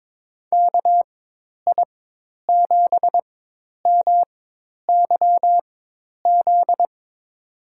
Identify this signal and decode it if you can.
Morse code, heard as a keyed tone, about 22 wpm, spaced as for 10 wpm, 700 Hz. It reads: KI7MYZ
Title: USA Call Signs - 6 Characters - 4x Character Spacing - 22wpm